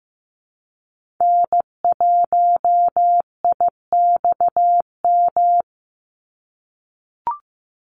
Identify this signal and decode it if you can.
Morse code, heard as a keyed tone, about 15 wpm, 700 Hz. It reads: N1IXM
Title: USA Call Signs 15wpm